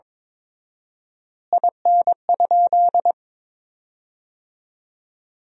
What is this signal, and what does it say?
Morse code, heard as a keyed tone, about 22 wpm, 700 Hz. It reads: IN?
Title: 4x Word-Spacing - Sentences from Top 100 Words - 22wpm